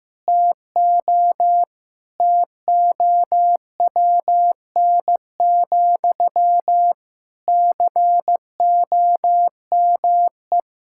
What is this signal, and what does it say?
Morse code, heard as a keyed tone, about 16 wpm, 700 Hz. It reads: TO TOWN, COME
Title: Sentences from Top 1000 Words - Review Part 6 - 15wpm